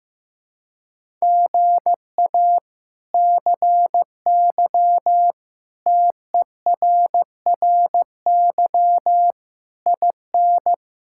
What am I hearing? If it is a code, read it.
Morse code, heard as a keyed tone, about 15 wpm, 700 Hz. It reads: GA CY TERRY IN
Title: K1USN SST QSO Practice - 15wpm